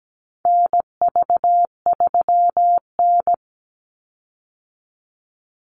Morse code, heard as a keyed tone, about 17 wpm, 700 Hz. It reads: NV3N
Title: Call Signs Repeated 2x - 17wpm